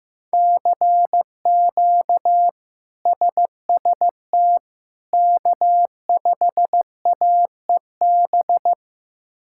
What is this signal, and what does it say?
Morse code, heard as a keyed tone, about 15 wpm, 700 Hz. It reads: CQ SST K5AEB
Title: K1USN SST QSO Practice - 15wpm